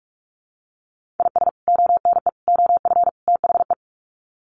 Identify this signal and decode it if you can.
Morse code, heard as a keyed tone, about 44 wpm, 700 Hz. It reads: IS ONE OF THE
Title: Sets of 4 Words 45wpm